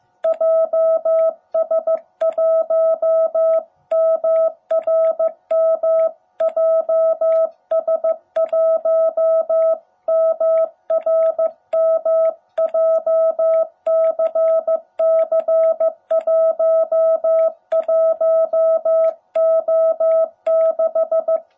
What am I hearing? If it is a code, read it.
Morse code, heard as a keyed tone, about 15 wpm, 660 Hz. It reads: JS1MRM JS1MRM JCC11O6
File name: jh8cbh / 2021cw